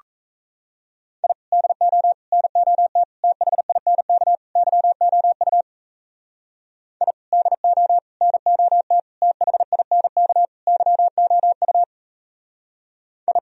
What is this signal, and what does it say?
Morse code, heard as a keyed tone, about 34 wpm, 700 Hz. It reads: I DO NOT THINK YOU I DO NOT THINK YOU I
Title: Speed-Racing - Sets of 5 Words 28wpm